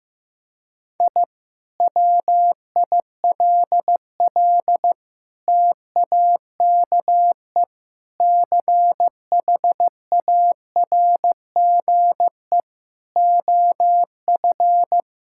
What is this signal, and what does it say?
Morse code, heard as a keyed tone, about 15 wpm, 700 Hz. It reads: I WILL TAKE CHARGE OF
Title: Sentences from Top 1000 Words - Review Part 6 - 15wpm